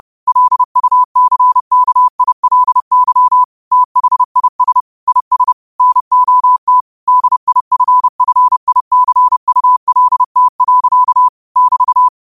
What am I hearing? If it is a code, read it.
Morse code, heard as a keyed tone, about 30 wpm, 1000 Hz. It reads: LUCKILY THIS IS NOT DIFFICULT. <BT>
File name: qotdfiles / 30wpm